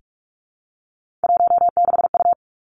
Morse code, heard as a keyed tone, about 46 wpm, 700 Hz. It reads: 16U
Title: Three Challenging Letter-Number Combinations 45wpm